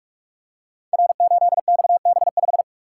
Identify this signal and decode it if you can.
Morse code, heard as a keyed tone, about 44 wpm, 700 Hz. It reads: R8XB5
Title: Code Groups - 5 Letter:Numbers - 45wpm